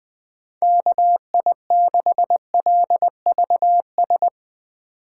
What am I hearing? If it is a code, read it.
Morse code, heard as a keyed tone, about 20 wpm, 700 Hz. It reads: KI6LVS
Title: USA Call Signs - 2x Character Spacing - 20wpm